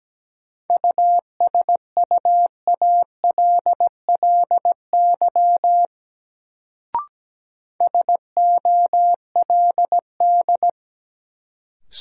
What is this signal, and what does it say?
Morse code, heard as a keyed tone, about 17 wpm, 700 Hz. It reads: USUALLY SOLD
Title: Sentences from Top 1400 Words - New Word Drill Alt - 17wpm